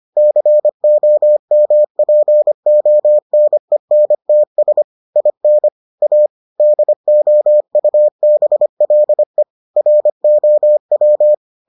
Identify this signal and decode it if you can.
Morse code, heard as a keyed tone, about 25 wpm, 600 Hz. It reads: COMPONENTS IN A DOUBLE ROW